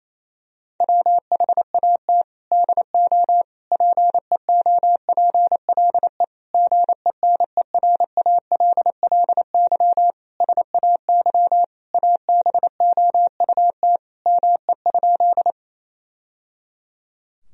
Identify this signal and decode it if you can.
Morse code, heard as a keyed tone, about 28 wpm, 700 Hz. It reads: WHAT DO PEOPLE GENERALLY SAY ABOUT ME?
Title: Sentences from Top 1000 Words - Review Part 4 - 28wpm